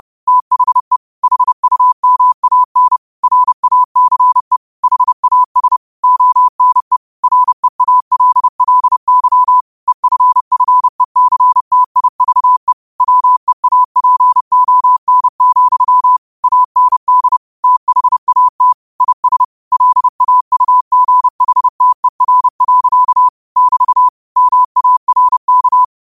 Morse code, heard as a keyed tone, about 30 wpm, 1000 Hz. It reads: THE HUMAN RACE HAS ONE REALLY EFFECTIVE WEAPON, AND THAT IS LAUGHTER. <BT> MARK
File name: qotdfiles / 30wpm